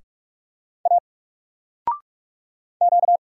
Morse code, heard as a keyed tone, about 45 wpm, 700 Hz. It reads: A Q
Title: Single Letter-Number 45wpm